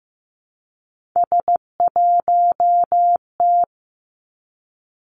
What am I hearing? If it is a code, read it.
Morse code, heard as a keyed tone, about 15 wpm, 700 Hz. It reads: S1T
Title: Three Challenging Letter-Number Combinations 15wpm